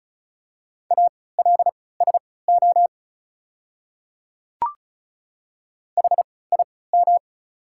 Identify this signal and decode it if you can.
Morse code, heard as a keyed tone, about 36 wpm, spaced as for 20 wpm, 690 Hz. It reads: ALSO HIM
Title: 3x Character Spacing - Top 200 Words - 35wpm